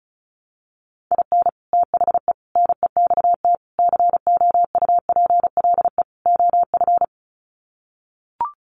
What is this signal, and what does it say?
Morse code, heard as a keyed tone, about 34 wpm, 700 Hz. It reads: IN THE NEXT COUPLE OF
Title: Sets of 5 Words Encore2 35wpm